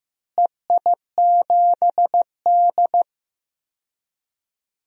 Morse code, heard as a keyed tone, about 15 wpm, 700 Hz. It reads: EI7D
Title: DXCC-100 Call Signs - 4 Characters - Repeated 2x - 15wpm